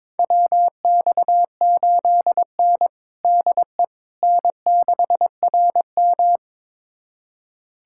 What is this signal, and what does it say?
Morse code, heard as a keyed tone, about 22 wpm, 700 Hz. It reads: WX8N DE N6RM
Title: USA Callsign DE Callsign - 4 Characters - 22wpm